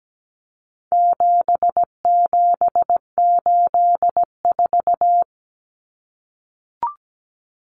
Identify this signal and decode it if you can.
Morse code, heard as a keyed tone, about 17 wpm, 700 Hz. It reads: 7784
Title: Common Cut Numbers - 4-Digit 17wpm